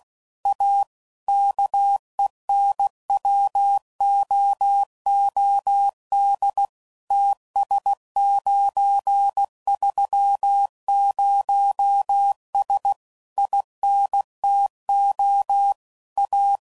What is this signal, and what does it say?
Morse code, heard as a keyed tone, about 16 wpm, 800 Hz. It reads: A KENWOOD TS930S INTO A
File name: CW Code Groups @15WPM / QSOs / kc7haa